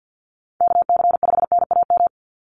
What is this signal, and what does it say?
Morse code, heard as a keyed tone, about 49 wpm, 700 Hz. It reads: KC5NAM
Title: USA Call Signs 50wpm